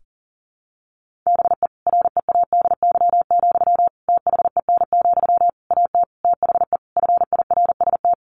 Morse code, heard as a keyed tone, about 40 wpm, 700 Hz. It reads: BE READY, THEN, AT THE FIRST
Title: Sentences from Top 1400 Words 40wpm